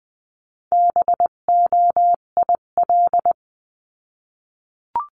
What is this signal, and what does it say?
Morse code, heard as a keyed tone, about 20 wpm, 700 Hz. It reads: BOIL
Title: 258 Lesson 42c - Words 16-20wpm